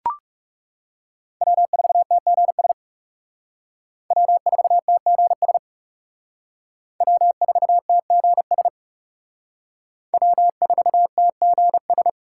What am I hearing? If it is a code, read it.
Morse code, heard as a keyed tone, about 34 wpm, 700 Hz. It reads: W4TGS W4TGS W4TGS W4TGS
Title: Speed Racing - USA 1x3 Call Signs 30wpm